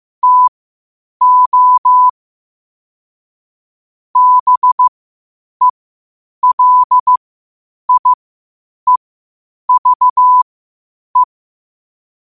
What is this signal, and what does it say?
Morse code, heard as a keyed tone, about 15 wpm, spaced as for 8 wpm, 1000 Hz. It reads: TO BELIEVE
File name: qotdfiles / 8wpm